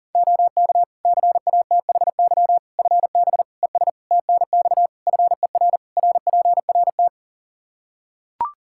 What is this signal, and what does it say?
Morse code, heard as a keyed tone, about 40 wpm, 700 Hz. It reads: OK CATHY FB ES TNX FER RPRT
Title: Practice QSO - Exchange 2 - Info - 40wpm